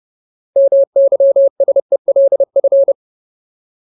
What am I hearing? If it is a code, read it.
Morse code, heard as a keyed tone, about 30 wpm, 550 Hz. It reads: MYSELF
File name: Ultimate Word List / 30WPM Ultimate